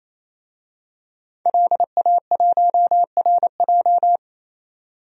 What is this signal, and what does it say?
Morse code, heard as a keyed tone, about 28 wpm, 700 Hz. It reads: LA1RJ
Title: DXCC-100 Call Signs - 5 Characters - 28wpm